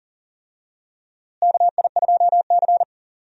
Morse code, heard as a keyed tone, about 40 wpm, 700 Hz. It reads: KI2C
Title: USA Call Signs - 4 Characters - Repeated 2x - 40wpm